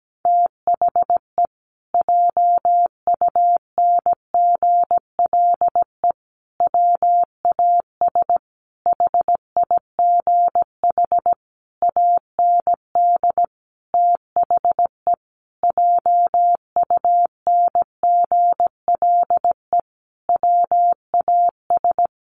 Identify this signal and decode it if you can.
Morse code, heard as a keyed tone, about 17 wpm, 700 Hz. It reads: THE JUNGLE WAS HIGH AND THE JUNGLE WAS
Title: A Sound Of Thunder 17wpm